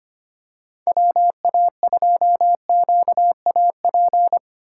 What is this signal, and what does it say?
Morse code, heard as a keyed tone, about 25 wpm, 700 Hz. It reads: WA2QAP
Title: USA Call Signs - 6 Characters - 3x Character Spacing - 25wpm